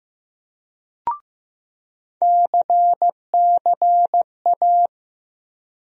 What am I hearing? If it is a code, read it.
Morse code, heard as a keyed tone, about 15 wpm, 700 Hz. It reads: CCA
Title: USA Call Sign Postfixes - 3 Letters 15wpm